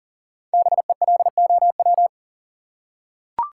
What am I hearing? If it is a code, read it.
Morse code, heard as a keyed tone, about 40 wpm, 700 Hz. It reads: BELOW
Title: Five Letter Words - 40wpm